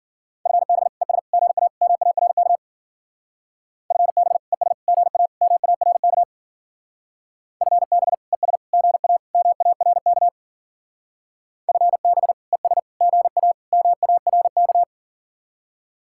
Morse code, heard as a keyed tone, about 49 wpm, 700 Hz. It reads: FB ES GA MARK FB ES GA MARK FB ES GA MARK FB ES GA MARK
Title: Speed-Racing Common QSO Phrases - 40wpm